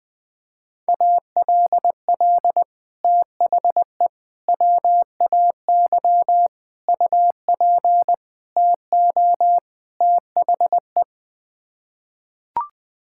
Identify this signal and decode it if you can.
Morse code, heard as a keyed tone, about 20 wpm, 700 Hz. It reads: ALL THE WAY UP TO THE
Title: Sets of 6 Words - 20wpm